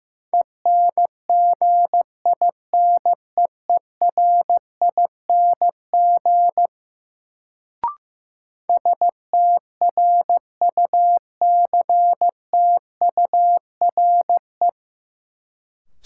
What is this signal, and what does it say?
Morse code, heard as a keyed tone, about 15 wpm, 700 Hz. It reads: ENGINEERING STRUCTURE
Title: Sentences from Top 1500 Words - New Word Drill Alt - 15wpm